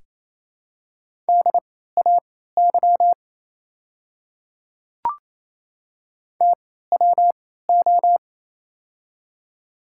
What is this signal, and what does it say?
Morse code, heard as a keyed tone, about 28 wpm, spaced as for 16 wpm, 700 Hz. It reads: DAY TWO
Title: 3x Character Spacing - Top 200 Words - 28wpm